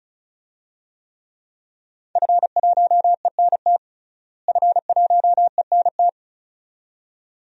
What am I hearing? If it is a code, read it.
Morse code, heard as a keyed tone, about 34 wpm, 700 Hz. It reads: F1ENT F1ENT
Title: DXCC-100 Call Signs - 5 Characters - Repeated 2x - 35wpm